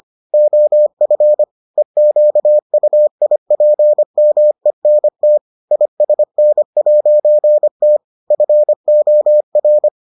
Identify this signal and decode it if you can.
Morse code, heard as a keyed tone, about 25 wpm, 600 Hz. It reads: OF EQUIPMENT ISN'T FOR